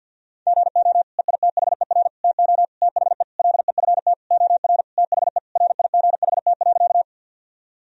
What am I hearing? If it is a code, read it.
Morse code, heard as a keyed tone, about 50 wpm, 700 Hz. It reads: GO EITHER TO THE LEFT OR THE RIGHT.
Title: Sentences from Top 1000 Words - Review Part 4 - 50wpm